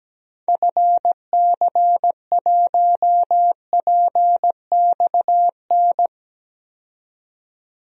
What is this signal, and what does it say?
Morse code, heard as a keyed tone, about 17 wpm, 700 Hz. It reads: FC1PXN
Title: DXCC-100 Call Signs - 6 Characters - 17wpm